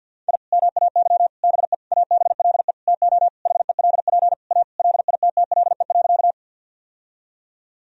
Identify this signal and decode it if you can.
Morse code, heard as a keyed tone, about 45 wpm, 700 Hz. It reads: I MAY BE ABLE TO HELP A LITTLE.
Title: Sentences from Top 1000 Words - Review Part 2 - 50wpm